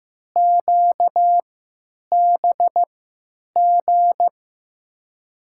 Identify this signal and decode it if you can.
Morse code, heard as a keyed tone, about 15 wpm, spaced as for 9 wpm, 700 Hz. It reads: QBG
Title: USA Call Signs - 6 Characters - 3x Character Spacing - 15wpm